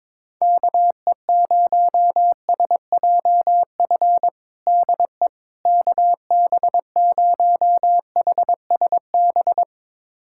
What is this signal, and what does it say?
Morse code, heard as a keyed tone, about 22 wpm, 700 Hz. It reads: KE0SJF DE KB0HSB
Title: USA Callsign DE Callsign - 6 Characters - 22wpm